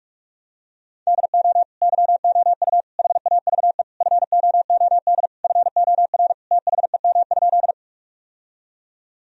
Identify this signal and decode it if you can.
Morse code, heard as a keyed tone, about 46 wpm, 700 Hz. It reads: DO YOU HAVE FOOD FOR THEM?
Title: Sentences from Top 1000 Words - Review Part 1 - 45wpm